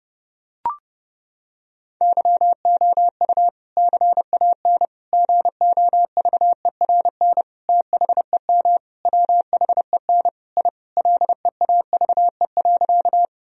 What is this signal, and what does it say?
Morse code, heard as a keyed tone, about 32 wpm, 700 Hz. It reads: YOU CAN GOVERN THEM WHEN I LEAVE.
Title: Sentences from Top 1000 Words - Review Part 2 - 30wpm